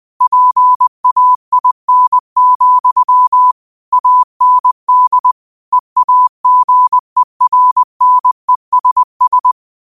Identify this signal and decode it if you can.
Morse code, heard as a keyed tone, about 20 wpm, 1000 Hz. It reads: PAIN, AND EAGERNESS
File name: qotdfiles / 20wpm